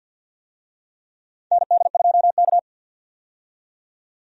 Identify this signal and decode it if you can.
Morse code, heard as a keyed tone, about 50 wpm, 700 Hz. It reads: ND2K
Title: USA Call Signs - 4 Characters - 50wpm